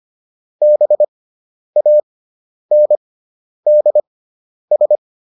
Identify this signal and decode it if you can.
Morse code, heard as a keyed tone, about 25 wpm, spaced as for 10 wpm, 600 Hz. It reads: BANDS